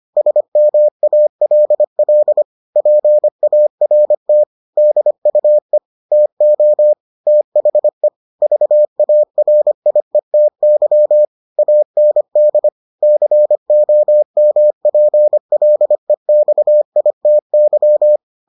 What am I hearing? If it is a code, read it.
Morse code, heard as a keyed tone, about 25 wpm, 600 Hz. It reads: SMALL PART DUE TO THE VARIETY AND COMPLEXITY